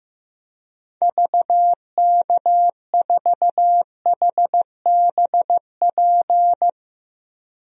Morse code, heard as a keyed tone, about 16 wpm, 700 Hz. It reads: VK4HBP
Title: DXCC-100 Call Signs - 6 Characters - 15wpm